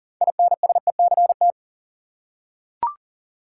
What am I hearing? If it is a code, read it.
Morse code, heard as a keyed tone, about 40 wpm, 700 Hz. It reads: INSECT
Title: Six Letter Words - 40wpm